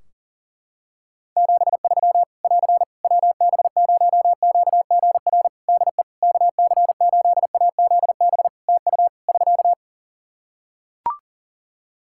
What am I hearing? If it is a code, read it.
Morse code, heard as a keyed tone, about 40 wpm, 700 Hz. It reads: 73 <AR> WB0QGR DE KC8AZB TU <SK>
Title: Practice QSO - Exchange 4 - Ending - 40wpm